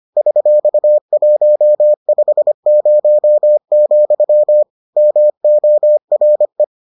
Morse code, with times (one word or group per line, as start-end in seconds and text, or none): 0.2-4.6 $150,
5.0-6.6 MORE